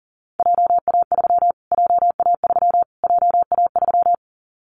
Morse code, heard as a keyed tone, about 40 wpm, 700 Hz. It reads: JA3 JA3 JA3
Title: Callsign Prefixes for Top 25 DXCC - Repeated 3x - 40wpm